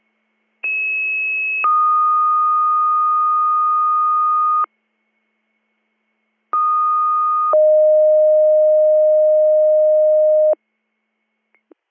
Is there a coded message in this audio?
Yes: two-tone page from 0.6 to 10.5 s, heard as a one-second tone, then a three-second tone, about 2600 then 1230 Hz, about 1230 then 620 Hz.